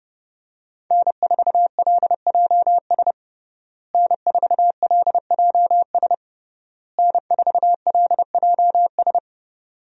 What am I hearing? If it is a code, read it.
Morse code, heard as a keyed tone, about 30 wpm, 700 Hz. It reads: N4LJS N4LJS N4LJS
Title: USA Call Signs - 5 Characters - Repeated 3x - 30wpm